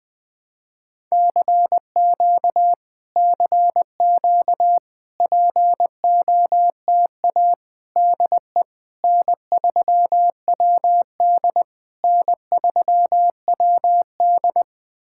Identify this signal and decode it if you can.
Morse code, heard as a keyed tone, about 20 wpm, 700 Hz. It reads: CQ CQ POTA DE N3WD N3WD
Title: POTA QSO Practice - 20wpm